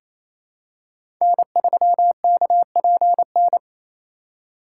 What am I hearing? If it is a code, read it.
Morse code, heard as a keyed tone, about 28 wpm, 700 Hz. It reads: N3KPN
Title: USA Call Signs - 4x Character Spacing - 28wpm